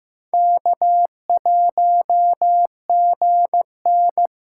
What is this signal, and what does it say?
Morse code, heard as a keyed tone, about 15 wpm, 700 Hz. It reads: K1GN
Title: USA Callsign DE Callsign - 4 Characters - 15wpm